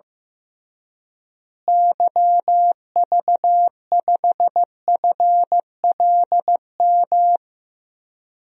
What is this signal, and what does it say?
Morse code, heard as a keyed tone, about 15 wpm, 700 Hz. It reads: YV5FLM
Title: DXCC-100 Call Signs - 6 Characters - Repeated 3x - 15wpm